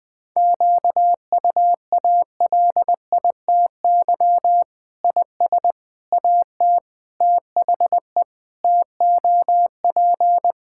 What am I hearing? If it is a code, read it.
Morse code, heard as a keyed tone, about 20 wpm, 700 Hz. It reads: QUALITY IS AT THE TOP
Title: Sentences with 6-Word Phrases - Spoken Beginning - 20wpm